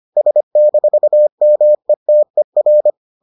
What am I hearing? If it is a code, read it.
Morse code, heard as a keyed tone, about 25 wpm, 600 Hz. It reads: S-METER